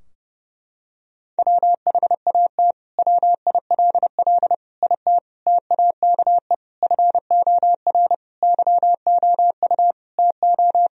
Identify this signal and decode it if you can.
Morse code, heard as a keyed tone, about 30 wpm, 700 Hz. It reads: WHAT WILL IT TAKE FOR YOU TO
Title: Sentences from Top 1000 Words - Review Part 3 - 30wpm